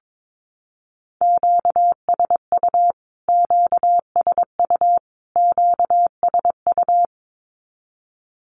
Morse code, heard as a keyed tone, about 22 wpm, 700 Hz. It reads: QSU QSU QSU
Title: Q-Codes Repeated 3x 22wpm